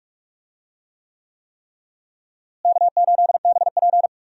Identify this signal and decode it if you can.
Morse code, heard as a keyed tone, about 44 wpm, 700 Hz. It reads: K8BP
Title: USA Call Signs - 4 Characters - Repeated 3x - 45wpm